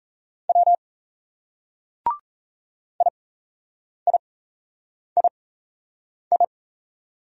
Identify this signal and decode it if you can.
Morse code, heard as a keyed tone, about 37 wpm, spaced as for 8 wpm, 700 Hz. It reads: W IIII